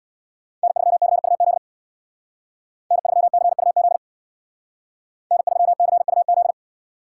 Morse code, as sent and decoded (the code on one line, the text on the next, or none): -. ...-- -.-. ..- --.. / -. ...-- -.-. ..- --.. / -. ...-- -.-. ..- --..
N3CUZ N3CUZ N3CUZ